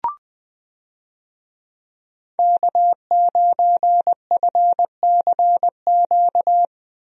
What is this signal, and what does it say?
Morse code, heard as a keyed tone, about 20 wpm, 700 Hz. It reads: K9FCQ
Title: USA Call Signs - 5 Characters - 20wpm